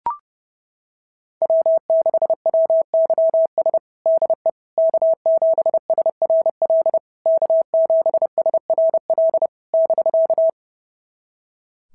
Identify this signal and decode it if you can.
Morse code, heard as a keyed tone, about 28 wpm, 650 Hz. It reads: W6WYS DE K7SRL K7SRL <BK>
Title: SOTA QSO Practice 30wpm